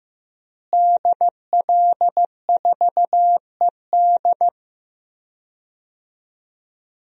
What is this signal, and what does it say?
Morse code, heard as a keyed tone, about 15 wpm, 700 Hz. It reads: DL4ED
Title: DXCC-100 Call Signs - 5 Characters - Repeated 2x - 15wpm